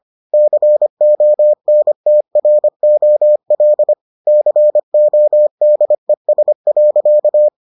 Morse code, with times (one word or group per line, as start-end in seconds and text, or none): 0.3-3.9 CONTROL
4.3-7.5 CODES.